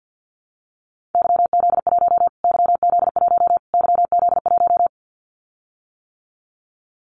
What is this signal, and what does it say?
Morse code, heard as a keyed tone, about 50 wpm, 700 Hz. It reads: YZ1 YZ1 YZ1
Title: Callsign Prefixes for Top 25 DXCC - Repeated 3x - 50wpm